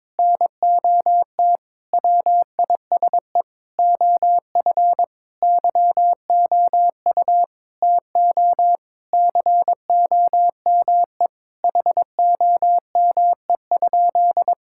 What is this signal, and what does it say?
Morse code, heard as a keyed tone, about 22 wpm, 700 Hz. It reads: NOT WISE OF YOU TO COME HOME?
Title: Sentences from Top 1600 Words 22wpm